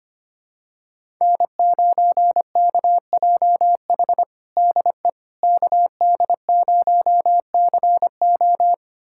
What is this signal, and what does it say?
Morse code, heard as a keyed tone, about 25 wpm, 700 Hz. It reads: N9KJH DE KD0CO
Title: USA Callsign DE Callsign - 5 Characters - 25wpm